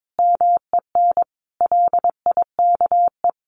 Morse code, heard as a keyed tone, about 22 wpm, 700 Hz. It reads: MEN LIKE
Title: A Sound Of Thunder 22wpm